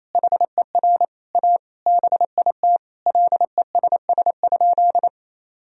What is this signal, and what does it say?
Morse code, heard as a keyed tone, about 28 wpm, 700 Hz. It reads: HER A BIT LESS?
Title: Sentences from Top 1000 Words - Review Part 2 - 28wpm